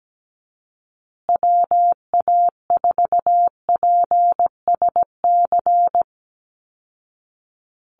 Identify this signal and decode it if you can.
Morse code, heard as a keyed tone, about 17 wpm, 700 Hz. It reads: WA4PSC
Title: Call Signs Repeated 2x - 17wpm